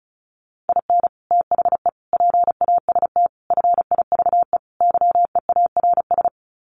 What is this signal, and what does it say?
Morse code, heard as a keyed tone, about 34 wpm, 700 Hz. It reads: IN THE PAST FIVE YEARS
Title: Sets of 5 Words Encore 35wpm